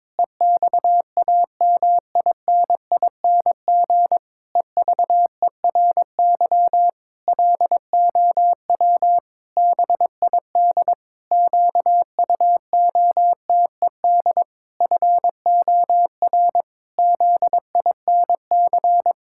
Morse code, heard as a keyed tone, about 21 wpm, 700 Hz. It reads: EXAMINING EVERY LOW BID QUOTED FOR ZINC
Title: Pangram Challenge v2 - 22wpm